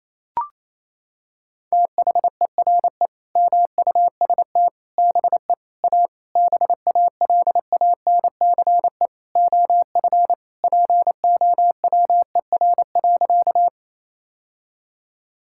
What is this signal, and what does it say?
Morse code, heard as a keyed tone, about 28 wpm, 700 Hz. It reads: THERE MUST BE A BALANCE OF POWER.
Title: Sentences from Top 1000 Words - Review Part 6 - 28wpm